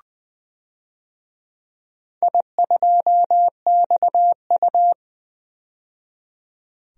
Morse code, heard as a keyed tone, about 21 wpm, 700 Hz. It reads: I2XU
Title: DXCC-100 Call Signs - 4 Characters - 20wpm